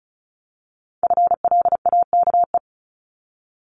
Morse code, heard as a keyed tone, about 36 wpm, 700 Hz. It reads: FLAKE
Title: Multi-letter Phonemes 35wpm